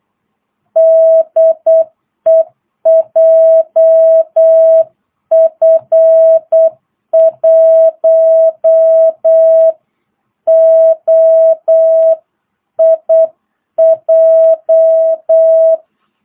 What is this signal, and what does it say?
Morse code, heard as a keyed tone, about 8 wpm, 660 Hz. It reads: DE JF1OIJ
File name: jh8cbh / 2022cw